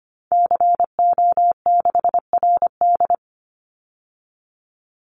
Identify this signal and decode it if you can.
Morse code, heard as a keyed tone, about 25 wpm, 700 Hz. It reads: CO6RD
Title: Call Signs Repeated 3x - 25wpm